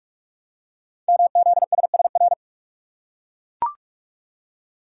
Morse code, heard as a keyed tone, about 44 wpm, 700 Hz. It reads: MZSSR
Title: Code Groups - 5 Letters - 45wpm